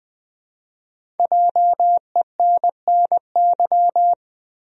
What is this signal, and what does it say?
Morse code, heard as a keyed tone, about 20 wpm, 700 Hz. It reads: JENNY
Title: USA Common Names - Max 5 Letters - 20wpm